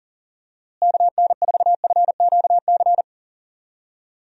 Morse code, heard as a keyed tone, about 40 wpm, 700 Hz. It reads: KN4FQC
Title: USA Call Signs - 6 Characters - Repeated 2x - 40wpm